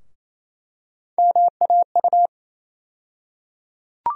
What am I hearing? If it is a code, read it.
Morse code, heard as a keyed tone, about 28 wpm, 700 Hz. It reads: MAU
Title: USA Call Sign Postfixes - 3 Letters 28wpm